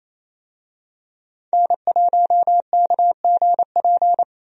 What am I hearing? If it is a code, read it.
Morse code, heard as a keyed tone, about 28 wpm, 700 Hz. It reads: N1KGP
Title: USA Call Signs - 5 Characters - Repeated 3x - 28wpm